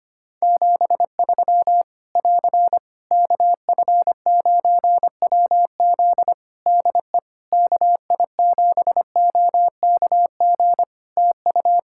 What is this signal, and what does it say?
Morse code, heard as a keyed tone, about 25 wpm, 700 Hz. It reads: 73 <AR> KF9WZ DE KI7OKG TU
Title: K1USN SST QSO Practice 25wpm